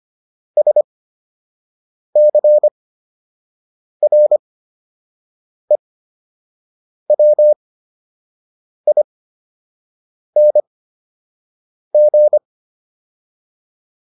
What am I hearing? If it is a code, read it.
Morse code, heard as a keyed tone, about 25 wpm, spaced as for 6 wpm, 600 Hz. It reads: SCREWING